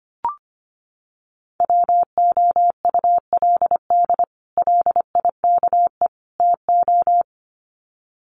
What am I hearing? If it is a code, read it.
Morse code, heard as a keyed tone, about 25 wpm, 700 Hz. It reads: WOULD LIKE TO
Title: Sets of 3 Words 25wpm